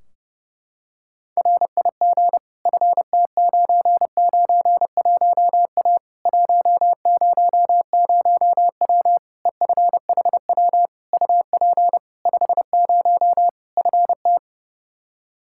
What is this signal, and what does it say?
Morse code, heard as a keyed tone, about 32 wpm, 700 Hz. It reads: RIG FT991A 100W EFHW UP 50 FT
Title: Practice QSO - Exchange 2 - Info - 30wpm